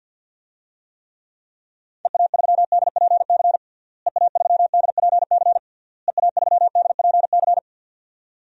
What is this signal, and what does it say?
Morse code, heard as a keyed tone, about 49 wpm, 700 Hz. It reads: EA3DPC EA3DPC EA3DPC
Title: DXCC-100 Call Signs - 6 Characters - Repeated 3x - 50wpm